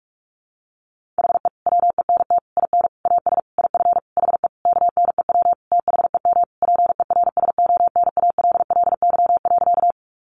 Morse code, heard as a keyed tone, about 44 wpm, 700 Hz. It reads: HE WENT IN AS IF HE KNEW THEM PERSONALLY.
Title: Sentences from Top 1400 Words 45wpm